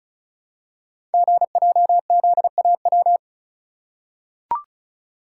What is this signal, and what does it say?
Morse code, heard as a keyed tone, about 36 wpm, 700 Hz. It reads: GJZAW
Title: Code Groups - 5 Letters - 35wpm